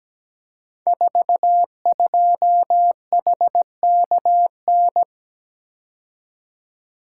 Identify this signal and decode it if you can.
Morse code, heard as a keyed tone, about 17 wpm, 700 Hz. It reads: 42HKN
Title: Code Groups - 5 Letter:Numbers - 17wpm